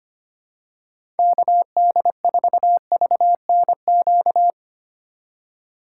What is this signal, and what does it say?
Morse code, heard as a keyed tone, about 25 wpm, 700 Hz. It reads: KD4VNQ
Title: USA Call Signs - 6 Characters - 2x Character Spacing - 25wpm